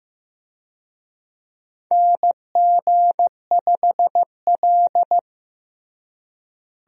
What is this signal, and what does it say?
Morse code, heard as a keyed tone, about 15 wpm, 700 Hz. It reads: NG5L
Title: USA Call Signs - 4 Characters - Repeated 2x - 15wpm